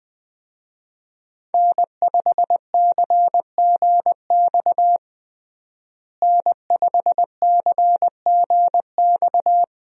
Morse code, heard as a keyed tone, about 20 wpm, 700 Hz. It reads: N5CGX N5CGX
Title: USA Call Signs - 5 Characters - Repeated 3x - 20wpm